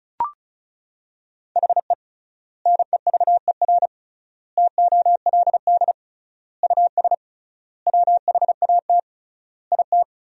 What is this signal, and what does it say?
Morse code, heard as a keyed tone, about 34 wpm, 700 Hz. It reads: HE NEVER TOLD US WHAT IT
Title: 3x Word-Spacing - Sentences from Top 400 Words - 35wpm